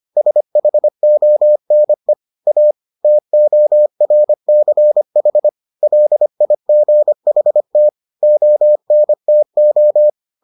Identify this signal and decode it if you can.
Morse code, heard as a keyed tone, about 25 wpm, 600 Hz. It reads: SHONE A TORCH LIGHT ONTO